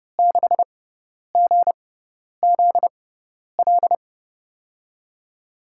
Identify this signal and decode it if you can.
Morse code, heard as a keyed tone, about 30 wpm, spaced as for 10 wpm, 700 Hz. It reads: 6GZL
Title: USA Call Signs - 6 Characters - 6x Character Spacing - 30wpm